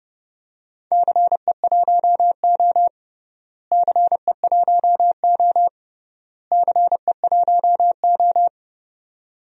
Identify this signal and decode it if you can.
Morse code, heard as a keyed tone, about 32 wpm, 700 Hz. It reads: CE1O CE1O CE1O
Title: DXCC-100 Call Signs - 4 Characters - Repeated 3x - 30wpm